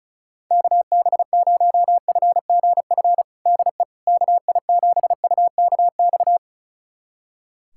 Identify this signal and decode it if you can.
Morse code, heard as a keyed tone, about 34 wpm, 700 Hz. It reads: KB0FGF DE KI7UKX
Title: USA Callsign DE Callsign - 6 Characters - 35wpm